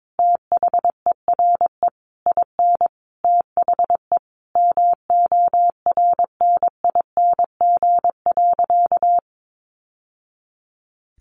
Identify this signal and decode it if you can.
Morse code, heard as a keyed tone, about 22 wpm, 700 Hz. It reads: THERE IN THE MORNING.
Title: Sentences from Top 1200 Words 22wpm